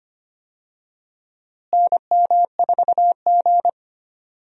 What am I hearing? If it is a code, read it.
Morse code, heard as a keyed tone, about 25 wpm, 700 Hz. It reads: NM4G
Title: USA Call Signs - 4 Characters - Repeated 2x - 25wpm